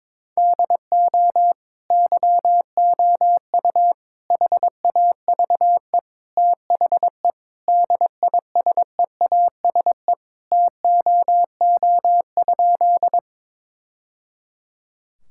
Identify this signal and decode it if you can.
Morse code, heard as a keyed tone, about 22 wpm, 700 Hz. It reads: DO YOU HAVE THE DISEASE TOO?
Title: Sentences from Top 1000 Words - Review Part 6 - 22wpm